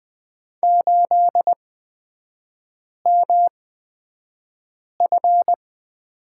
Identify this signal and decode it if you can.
Morse code, heard as a keyed tone, about 20 wpm, spaced as for 5 wpm, 700 Hz. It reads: 8MF
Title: 255 Lesson 41e - Period - Three Characters 20wpm